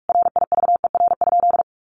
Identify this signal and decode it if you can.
Morse code, heard as a keyed tone, about 44 wpm, 700 Hz. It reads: RIVER?
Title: Sentences from Top 1400 Words 45wpm